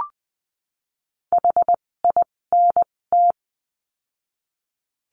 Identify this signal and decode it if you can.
Morse code, heard as a keyed tone, about 20 wpm, spaced as for 16 wpm, 700 Hz. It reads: HINT
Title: 256 Lesson 42a - Words 12-20wpm